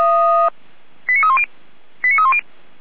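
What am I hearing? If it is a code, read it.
Five-tone selective call (ZVEI), heard as quick runs of five tones, tones of about 70 ms. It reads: 89310 89310
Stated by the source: ZVEI five-tone selective call 89299, received by SDR.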